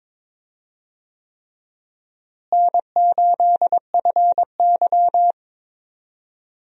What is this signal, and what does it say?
Morse code, heard as a keyed tone, about 21 wpm, 700 Hz. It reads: N8FY
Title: USA Call Signs - 4 Characters - Repeated 3x - 22wpm